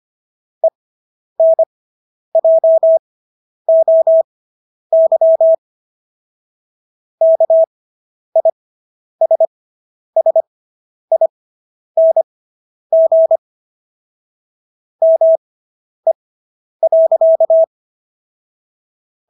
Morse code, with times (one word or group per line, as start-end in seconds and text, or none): 0.6-5.5 ENJOY
7.2-13.4 KISSING
15.0-17.6 ME.